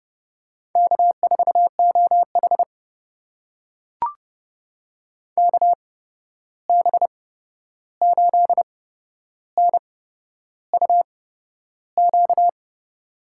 Morse code, heard as a keyed tone, about 30 wpm, spaced as for 11 wpm, 700 Hz. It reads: K4OH KB8NUQ
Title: USA Call Signs - 8x Character Spacing - 30wpm